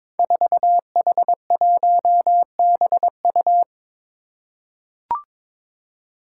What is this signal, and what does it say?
Morse code, heard as a keyed tone, about 22 wpm, 700 Hz. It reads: 4H1BU
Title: Five Challenging Letter-Number Combinations 22wpm